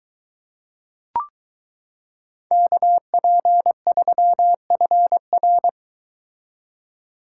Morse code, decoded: KP3FR

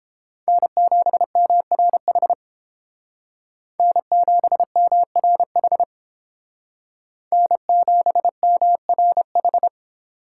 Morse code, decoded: N7MRH N7MRH N7MRH

